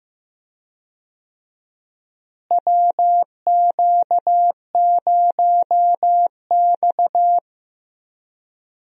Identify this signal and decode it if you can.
Morse code, heard as a keyed tone, about 15 wpm, 700 Hz. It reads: WQ0X
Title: USA Call Signs - 4 Characters - Repeated 3x - 15wpm